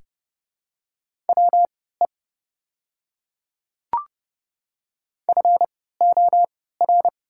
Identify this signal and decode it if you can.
Morse code, heard as a keyed tone, about 31 wpm, spaced as for 17 wpm, 700 Hz. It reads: WE FOR
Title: 3x Character Spacing - Top 200 Words - 30wpm